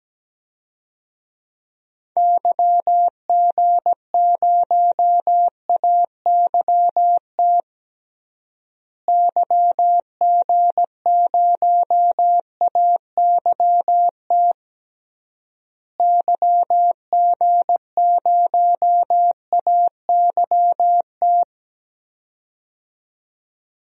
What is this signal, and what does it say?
Morse code, heard as a keyed tone, about 17 wpm, 700 Hz. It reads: YG0AYT YG0AYT YG0AYT
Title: DXCC-100 Call Signs - 6 Characters - Repeated 3x - 17wpm